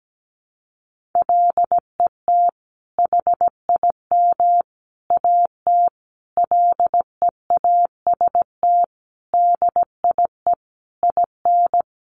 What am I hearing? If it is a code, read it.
Morse code, heard as a keyed tone, about 17 wpm, 700 Hz. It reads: LET HIM AT LEAST DIE IN
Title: Sentences from Top 1500 Words 17wpm